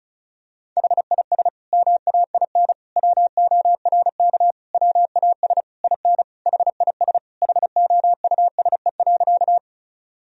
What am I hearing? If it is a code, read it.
Morse code, heard as a keyed tone, about 36 wpm, 700 Hz. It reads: HIS MAIN WORK WAS IN HIS HOUSE.
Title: Sentences from Top 1000 Words - Review Part 2 - 35wpm